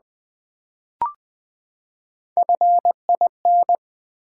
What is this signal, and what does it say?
Morse code, heard as a keyed tone, about 20 wpm, 700 Hz. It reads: FIN
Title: Build A Word - 3 Letters At A Time - 9-Letter Words - 20wpm